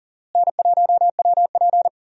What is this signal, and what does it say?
Morse code, heard as a keyed tone, about 40 wpm, 700 Hz. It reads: N1WP
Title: USA Call Signs - 4 Characters - Repeated 3x - 40wpm